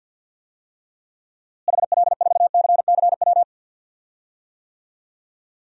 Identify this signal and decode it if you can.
Morse code, heard as a keyed tone, about 50 wpm, 700 Hz. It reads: HL4CCW
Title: DXCC-100 Call Signs - 6 Characters - Repeated 3x - 50wpm